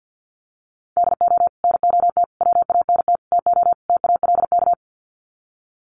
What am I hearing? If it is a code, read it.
Morse code, heard as a keyed tone, about 48 wpm, 700 Hz. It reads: DO NOT WANT TO TALK
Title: Sets of 5 Words 50wpm